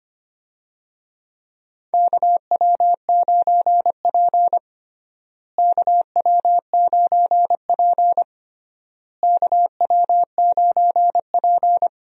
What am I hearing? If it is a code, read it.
Morse code, heard as a keyed tone, about 25 wpm, 700 Hz. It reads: KW9P KW9P KW9P
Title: USA Call Signs - 4 Characters - Repeated 3x - 25wpm